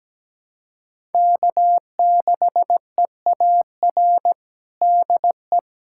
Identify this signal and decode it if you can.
Morse code, heard as a keyed tone, about 17 wpm, 700 Hz. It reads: K6EAR DE
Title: USA Callsign DE Callsign - 5 Characters - 17wpm